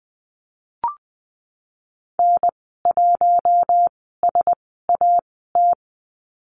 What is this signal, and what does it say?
Morse code, heard as a keyed tone, about 20 wpm, spaced as for 15 wpm, 700 Hz. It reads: N1SAT
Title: 048 Lesson 9g - Number 4 - Call Signs 20wpm